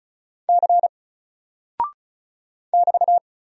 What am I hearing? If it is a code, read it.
Morse code, heard as a keyed tone, about 36 wpm, 700 Hz. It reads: C <BT>
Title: Single Character (Letter:Number:Punctuation:Prosign) Alt 35wpm